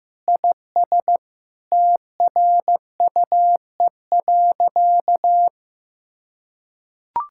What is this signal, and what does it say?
Morse code, heard as a keyed tone, about 15 wpm, 700 Hz. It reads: IS TRUE.
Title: Sentences from Top 1600 Words 15wpm